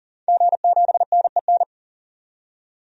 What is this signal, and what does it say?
Morse code, heard as a keyed tone, about 40 wpm, 700 Hz. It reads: G7NEN
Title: DXCC-100 Call Signs - 5 Characters - 40wpm